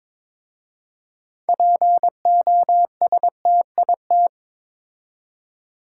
Morse code, heard as a keyed tone, about 22 wpm, 700 Hz. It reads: POSTIT
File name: Difficult Words with Hints - 22wpm